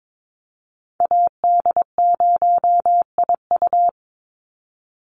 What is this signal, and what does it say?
Morse code, heard as a keyed tone, about 22 wpm, 700 Hz. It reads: AD0IU
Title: Call Signs - 22wpm